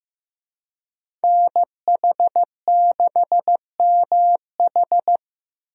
Morse code, heard as a keyed tone, about 15 wpm, 700 Hz. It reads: NH6MH
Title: USA Call Signs - 5 Characters - Repeated 2x - 15wpm